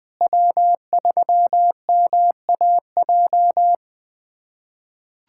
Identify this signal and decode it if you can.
Morse code, heard as a keyed tone, about 21 wpm, 700 Hz. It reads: W3MAJ
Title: USA Callsign DE Callsign - 5 Characters - 20wpm